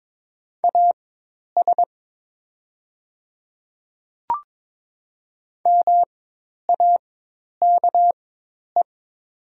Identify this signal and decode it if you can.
Morse code, heard as a keyed tone, about 22 wpm, spaced as for 10 wpm, 700 Hz. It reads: AS MAKE
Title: 4x Character Spacing - Top 100 Words - 22wpm